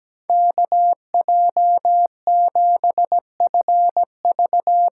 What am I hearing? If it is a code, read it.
Morse code, heard as a keyed tone, about 17 wpm, 700 Hz. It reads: KJ7FV
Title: USA Call Signs - 8x Character Spacing - 17wpm